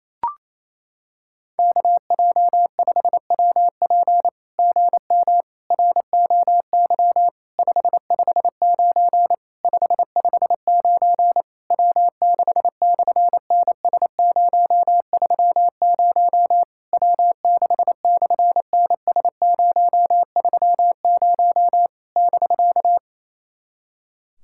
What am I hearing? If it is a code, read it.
Morse code, heard as a keyed tone, about 28 wpm, 700 Hz. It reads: KJ5WP GM ROY 559 559 W6/NS030 W6/NS030 <BK>
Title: SOTA QSO Practice 28wpm